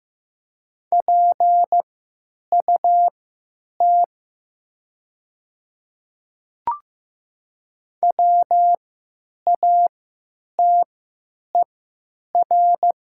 Morse code, heard as a keyed tone, about 15 wpm, spaced as for 9 wpm, 700 Hz. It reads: PUT WATER
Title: 3x Character Spacing - Top 200 Words - 15wpm